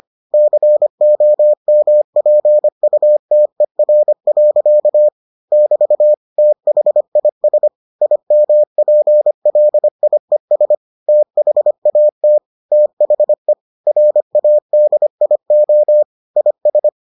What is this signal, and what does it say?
Morse code, heard as a keyed tone, about 25 wpm, 600 Hz. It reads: COMPUTER. <BT> THIS IMPLIES THAT THE RADIO IS